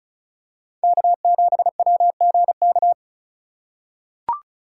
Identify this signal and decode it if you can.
Morse code, heard as a keyed tone, about 36 wpm, 700 Hz. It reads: K7WGK
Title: USA Call Signs - 5 Characters - 2x Character Spacing - 35wpm